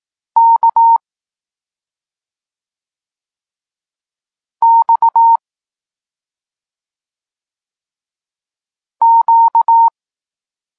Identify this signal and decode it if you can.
Morse code, heard as a keyed tone, about 18 wpm, 920 Hz. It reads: K X Q
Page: Tekst 32 – Doskonalenie odbioru poznanych liter